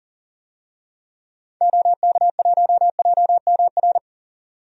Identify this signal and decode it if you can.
Morse code, heard as a keyed tone, about 40 wpm, 700 Hz. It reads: OK1JMR
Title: DXCC-100 Call Signs - 6 Characters - 40wpm